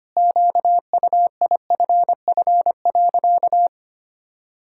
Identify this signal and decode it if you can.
Morse code, heard as a keyed tone, about 25 wpm, 700 Hz. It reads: QUIFF.
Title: Pangram Challenge v2 - 25wpm